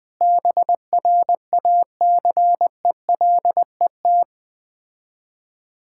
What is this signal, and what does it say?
Morse code, heard as a keyed tone, about 20 wpm, 700 Hz. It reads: BRACELET
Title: Difficult Words - Speed-Racing with Hints - 20wpm